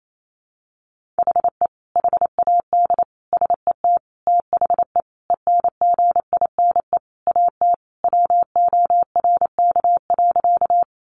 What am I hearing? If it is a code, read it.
Morse code, heard as a keyed tone, about 28 wpm, 700 Hz. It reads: HE HAD SET THE ENGINE AT WORK.
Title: Sentences from Top 1500 Words 28wpm